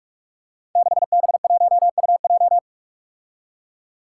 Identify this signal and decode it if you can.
Morse code, heard as a keyed tone, about 44 wpm, 700 Hz. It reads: 6B1UJ
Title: Five Challenging Letter-Number Combinations 45wpm